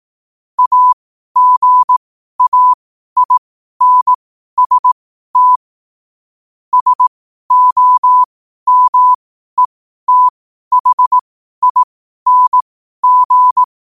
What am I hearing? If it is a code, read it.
Morse code, heard as a keyed tone, about 18 wpm, spaced as for 12 wpm, 1000 Hz. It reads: AGAINST SOMETHING
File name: qotdfiles / 12wpm